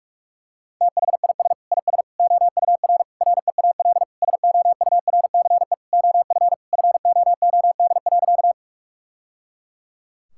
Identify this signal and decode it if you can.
Morse code, heard as a keyed tone, about 41 wpm, 700 Hz. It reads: THIS IS OUR REAL SOURCE OF FOOD.
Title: Sentences from Top 1000 Words - Review Part 5 - 45wpm